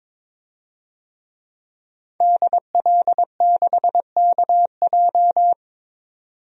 Morse code, heard as a keyed tone, about 22 wpm, 700 Hz. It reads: DL6KJ
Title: DXCC-100 Call Signs - 5 Characters - 22wpm